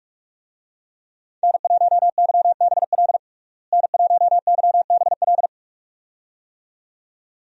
Morse code, heard as a keyed tone, about 48 wpm, 700 Hz. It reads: N1YBL N1YBL
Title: USA Call Signs - 5 Characters - Repeated 2x - 45wpm